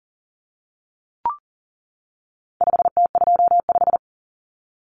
Morse code, heard as a keyed tone, about 40 wpm, 700 Hz. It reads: 5T25
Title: Common Cut Numbers - 4-Digit 40wpm